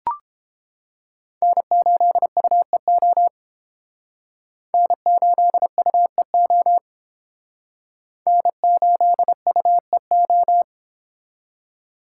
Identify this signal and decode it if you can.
Morse code, heard as a keyed tone, about 29 wpm, 700 Hz. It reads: N8UEO N8UEO N8UEO